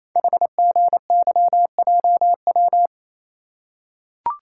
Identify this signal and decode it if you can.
Morse code, heard as a keyed tone, about 28 wpm, 700 Hz. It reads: HGYJW
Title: Code Groups - 5 Letters - 28wpm